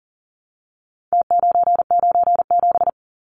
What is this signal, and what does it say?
Morse code, heard as a keyed tone, about 40 wpm, 700 Hz. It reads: T997
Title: Numbers - Contest Style - 4-Digit 40wpm